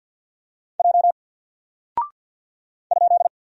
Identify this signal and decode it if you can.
Morse code, heard as a keyed tone, about 50 wpm, 700 Hz. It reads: J ?
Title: Single Character (Letter:Number:Punctuation:Prosign) 50wpm